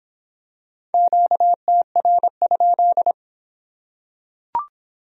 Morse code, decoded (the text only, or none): QTR?